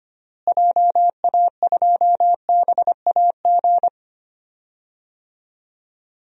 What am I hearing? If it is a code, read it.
Morse code, heard as a keyed tone, about 25 wpm, 700 Hz. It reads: JA2BAG
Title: DXCC-100 Call Signs - 6 Characters - Repeated 3x - 25wpm